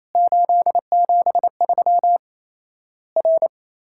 Morse code, heard as a keyed tone, about 27 wpm, 700 Hz. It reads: 873 R
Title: Worked All Europe DX Contest - QTC Exchange - 28wpm